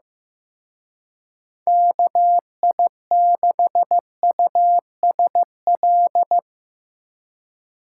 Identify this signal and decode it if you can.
Morse code, heard as a keyed tone, about 16 wpm, 700 Hz. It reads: KI6USL
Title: Speed Racing - USA 2x3 Call Signs 15wpm